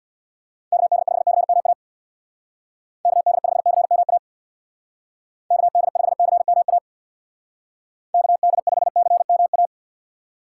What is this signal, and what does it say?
Morse code, heard as a keyed tone, about 60 wpm, 700 Hz. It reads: KD5CMA KD5CMA KD5CMA KD5CMA
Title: Speed Racing - USA 2x3 Call Signs 50wpm